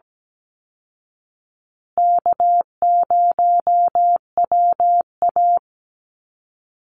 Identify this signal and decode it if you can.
Morse code, heard as a keyed tone, about 17 wpm, 700 Hz. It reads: K0WA
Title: Call Signs Repeated 3x - 17wpm